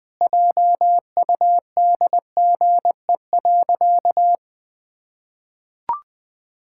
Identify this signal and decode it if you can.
Morse code, heard as a keyed tone, about 21 wpm, 700 Hz. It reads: JUDGE.
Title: Sentences with 3-Word Phrases - 20wpm